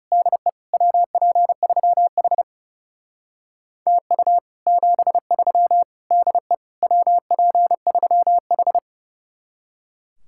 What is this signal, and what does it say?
Morse code, heard as a keyed tone, about 36 wpm, 700 Hz. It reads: DE WP3H TU 73 DE WP3H
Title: Speed-Racing Common QSO Phrases Encore - 30wpm